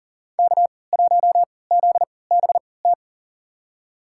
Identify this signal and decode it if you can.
Morse code, heard as a keyed tone, about 40 wpm, spaced as for 23 wpm, 700 Hz. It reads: K1ZBT
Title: USA Call Signs - 5 Characters - 3x Character Spacing - 40wpm